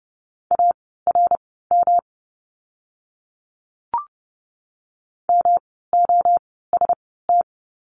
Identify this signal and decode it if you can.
Morse code, heard as a keyed tone, about 30 wpm, spaced as for 17 wpm, 700 Hz. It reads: ARM MOST
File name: course / 30:10wpm / 108 Lesson 18f - Letter m - Words 30wpm